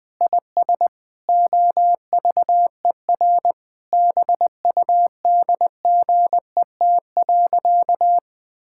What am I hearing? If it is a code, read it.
Morse code, heard as a keyed tone, about 20 wpm, 700 Hz. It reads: IS OVER BUDGET.
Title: Sentences with 8-Word Phrases - 20wpm